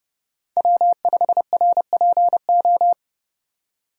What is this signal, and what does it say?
Morse code, heard as a keyed tone, about 30 wpm, 700 Hz. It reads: W5RPO
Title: USA Call Signs - 5 Characters - 2x Character Spacing - 30wpm